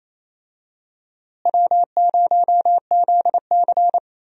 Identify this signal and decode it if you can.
Morse code, heard as a keyed tone, about 28 wpm, 700 Hz. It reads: W0ZC
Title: USA Call Signs - 4 Characters - Repeated 3x - 28wpm